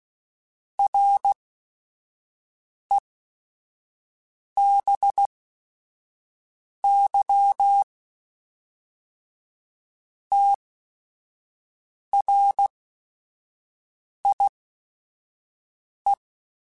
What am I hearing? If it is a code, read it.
Morse code, heard as a keyed tone, about 16 wpm, spaced as for 5 wpm, 800 Hz. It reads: RE BY TRIE
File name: Technical Subjects / Technical Subjects-mp3s / XL @5